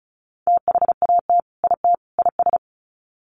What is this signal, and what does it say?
Morse code, heard as a keyed tone, about 34 wpm, 700 Hz. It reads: THAT IT IS
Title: Sets of 3 Words 35wpm